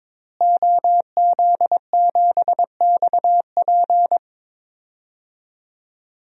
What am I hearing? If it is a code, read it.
Morse code, heard as a keyed tone, about 22 wpm, 700 Hz. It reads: OZ7XP
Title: DXCC-100 Call Signs - 5 Characters - Repeated 2x - 22wpm